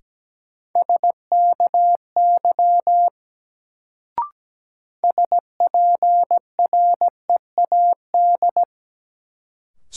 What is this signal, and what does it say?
Morse code, heard as a keyed tone, about 17 wpm, 700 Hz. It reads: SKY SPREAD
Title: Sentences from Top 1200 Words - New Word Drill Alt - 17wpm